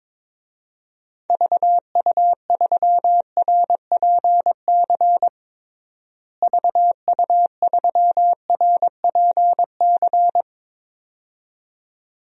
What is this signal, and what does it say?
Morse code, heard as a keyed tone, about 22 wpm, 700 Hz. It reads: VU3RPC VU3RPC
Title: DXCC-100 Call Signs - 6 Characters - Repeated 2x - 22wpm